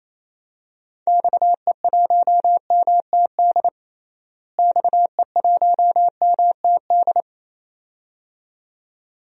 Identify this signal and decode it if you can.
Morse code, heard as a keyed tone, about 28 wpm, 700 Hz. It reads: XE1MTD XE1MTD
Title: DXCC-100 Call Signs - 6 Characters - Repeated 2x - 28wpm